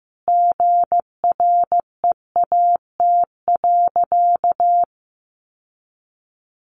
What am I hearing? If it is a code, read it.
Morse code, heard as a keyed tone, about 15 wpm, 700 Hz. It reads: GREAT.
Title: Sentences from Top 1400 Words 15wpm